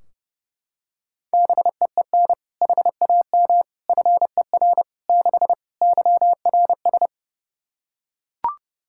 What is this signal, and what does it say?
Morse code, heard as a keyed tone, about 32 wpm, 700 Hz. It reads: BEEN HAM FER 6 YRS
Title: K1USN SST QSO Practice 30wpm